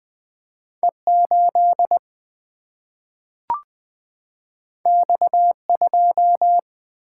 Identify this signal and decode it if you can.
Morse code, heard as a keyed tone, about 20 wpm, 700 Hz. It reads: E 8 X 2
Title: Single-Letter Single-Number 20wpm